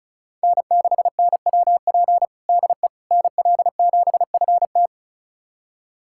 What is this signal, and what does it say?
Morse code, heard as a keyed tone, about 36 wpm, 700 Hz. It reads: N6NWP DE NL7FT
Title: USA Callsign DE Callsign - 5 Characters - 35wpm